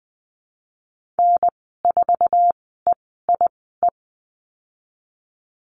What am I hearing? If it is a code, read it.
Morse code, heard as a keyed tone, about 20 wpm, spaced as for 15 wpm, 700 Hz. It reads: N4EIE
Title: 062 Lesson 11g - Letter h - Call Signs 20wpm